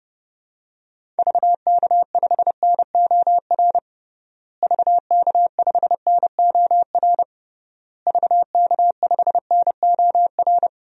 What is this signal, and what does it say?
Morse code, heard as a keyed tone, about 32 wpm, 700 Hz. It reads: VK5NOR VK5NOR VK5NOR
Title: DXCC-100 Call Signs - 6 Characters - Repeated 3x - 30wpm